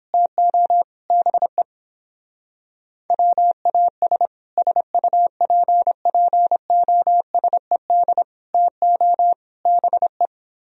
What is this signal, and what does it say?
Morse code, heard as a keyed tone, about 29 wpm, 700 Hz. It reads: TO BE WAS SUPPOSED TO BE